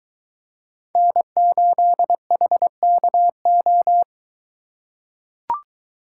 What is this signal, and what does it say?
Morse code, heard as a keyed tone, about 23 wpm, 700 Hz. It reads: N8HKO